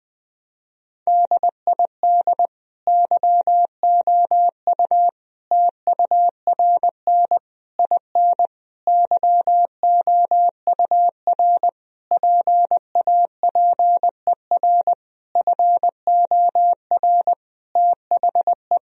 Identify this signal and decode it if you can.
Morse code, heard as a keyed tone, about 20 wpm, 700 Hz. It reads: DID YOU TURN IN YOUR PAPER FOR THE
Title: Sentences from Top 1000 Words - Review Part 2 - 20wpm